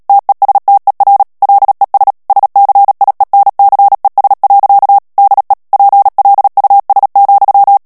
Morse code, heard as a keyed tone, about 37 wpm, 800 Hz. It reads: TESTER LES SCIENCES. DE PLUS,
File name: CW / FICHIERS-MP3 / FRANCAIS TEXTE CLAIR 1 VITESSE 35 MOTS MINUTES